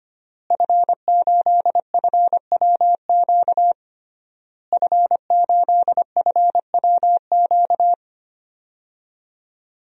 Morse code, heard as a keyed tone, about 25 wpm, 700 Hz. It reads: F8FWQ F8FWQ
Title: DXCC-100 Call Signs - 5 Characters - Repeated 3x - 25wpm